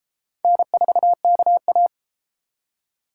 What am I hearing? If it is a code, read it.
Morse code, heard as a keyed tone, about 31 wpm, 700 Hz. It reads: N4KA